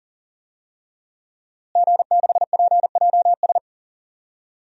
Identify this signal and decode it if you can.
Morse code, heard as a keyed tone, about 40 wpm, 700 Hz. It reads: G6PJS
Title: DXCC-100 Call Signs - 5 Characters - Repeated 2x - 40wpm